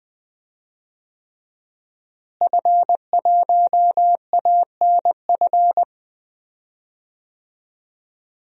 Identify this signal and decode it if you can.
Morse code, heard as a keyed tone, about 20 wpm, 700 Hz. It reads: F1ANF
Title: DXCC-100 Call Signs - 5 Characters - Repeated 3x - 20wpm